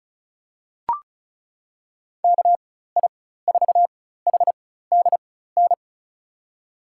Morse code, heard as a keyed tone, about 34 wpm, spaced as for 16 wpm, 700 Hz. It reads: KI4HDN
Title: USA Call Signs - 6 Characters - 4x Character Spacing - 35wpm